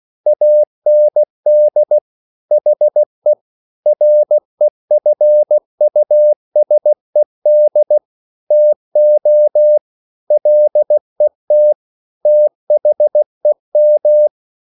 Morse code, heard as a keyed tone, about 16 wpm, 600 Hz. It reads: AND HE REFUSED TO LET THEM